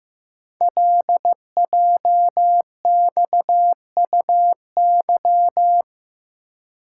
Morse code, heard as a keyed tone, about 15 wpm, 700 Hz. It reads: LJXUY
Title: Code Groups - 5 Letter:Numbers - 15wpm